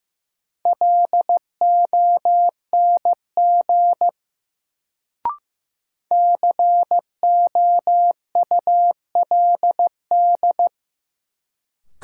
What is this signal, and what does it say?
Morse code, heard as a keyed tone, about 16 wpm, 700 Hz. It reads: LONG COULD
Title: Sentences from Top 100 Words - New Word Drill - 15wpm